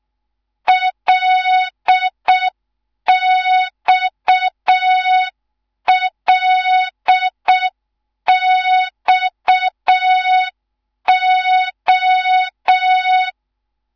Morse code, heard as a keyed tone, about 6 wpm, 3700 Hz. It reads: LXLXO
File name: Morze-HA2MN / hang / 11 GYAKORLAT